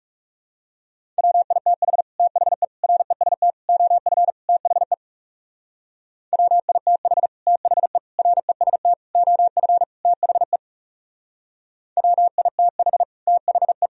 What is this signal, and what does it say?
Morse code, heard as a keyed tone, about 36 wpm, 700 Hz. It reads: WITH THE REST OF THE WITH THE REST OF THE WITH THE
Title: Speed-Racing - Sets of 5 Words 30wpm